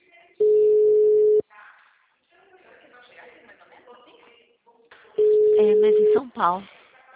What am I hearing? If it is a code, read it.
Ringback tone (Europe, 425 Hz), with speech.